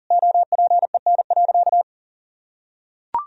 Morse code, heard as a keyed tone, about 40 wpm, 700 Hz. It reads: OPEN.
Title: Sentences from Top 1000 Words - Review Part 3 - 40wpm